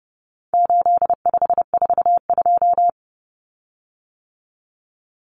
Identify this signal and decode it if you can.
Morse code, heard as a keyed tone, about 30 wpm, 700 Hz. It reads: 8542
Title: Common Cut Numbers - 4-Digit 30wpm